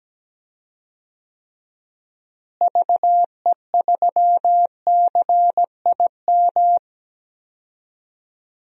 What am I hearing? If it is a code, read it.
Morse code, heard as a keyed tone, about 17 wpm, 700 Hz. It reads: VE3CIM
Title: DXCC-100 Call Signs - 6 Characters - Repeated 2x - 17wpm